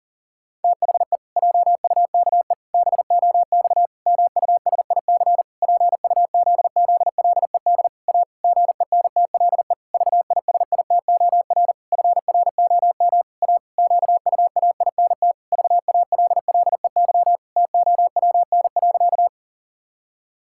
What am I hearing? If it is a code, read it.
Morse code, heard as a keyed tone, about 40 wpm, 700 Hz. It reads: THE JUKE BOX MUSIC PUZZLED A GENTLE VISITOR FROM A QUAINT VALLEY TOWN.
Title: Pangram Challenge v2 - 40wpm